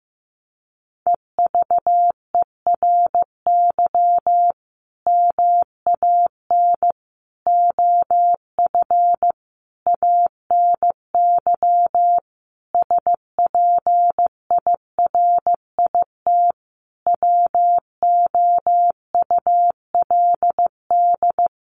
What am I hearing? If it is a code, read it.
Morse code, heard as a keyed tone, about 15 wpm, 700 Hz. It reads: EVERY MAN OF ANY SPIRIT WOULD
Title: Sentences from Top 1400 Words 15wpm